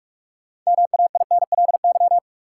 Morse code, heard as a keyed tone, about 46 wpm, 700 Hz. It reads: MAINLY
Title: Sentences from Top 1400 Words - New Word Drill - 45wpm